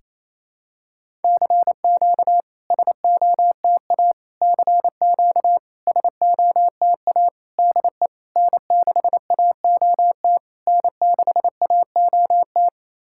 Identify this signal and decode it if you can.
Morse code, heard as a keyed tone, about 28 wpm, 690 Hz. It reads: CQ SOTA CQ SOTA DE N6AOT N6AOT
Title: SOTA QSO Practice 28wpm